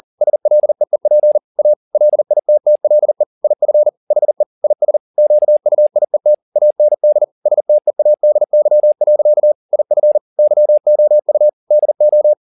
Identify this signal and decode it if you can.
Morse code, heard as a keyed tone, about 40 wpm, 600 Hz. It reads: SLEEP A LITTLE IF HE IS QUIET AND STEADY. IF YOU DO
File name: the old man and the sea 2 40wpm